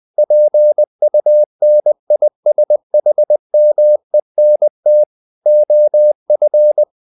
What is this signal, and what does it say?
Morse code, heard as a keyed tone, about 20 wpm, 600 Hz. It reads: PUNISHMENT OF
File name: the old man and the sea part 2 20wpm